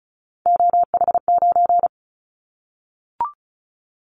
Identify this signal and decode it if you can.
Morse code, heard as a keyed tone, about 36 wpm, 700 Hz. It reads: OH9
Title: Callsign Prefixes for Top 50 DXCC - 35wpm